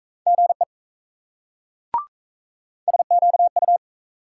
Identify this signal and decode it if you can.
Morse code, heard as a keyed tone, about 35 wpm, 700 Hz. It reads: GE SQU